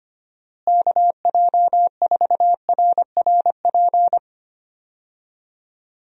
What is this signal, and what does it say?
Morse code, heard as a keyed tone, about 25 wpm, 700 Hz. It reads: KJ4RRP
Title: USA Call Signs - 6 Characters - Repeated 2x - 25wpm